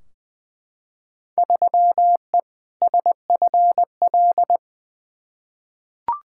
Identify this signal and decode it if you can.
Morse code, heard as a keyed tone, about 20 wpm, 700 Hz. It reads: 3E SFL
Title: Field Day Exchanges Encore - 20wpm